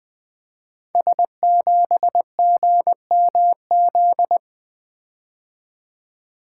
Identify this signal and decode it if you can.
Morse code, heard as a keyed tone, about 20 wpm, 700 Hz. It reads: S7GMZ
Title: Code Groups - 5 Letter:Numbers - 20wpm